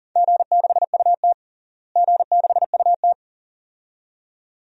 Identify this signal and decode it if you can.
Morse code, heard as a keyed tone, about 37 wpm, 700 Hz. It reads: G6UT G6UT
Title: DXCC-100 Call Signs - 4 Characters - Repeated 2x - 40wpm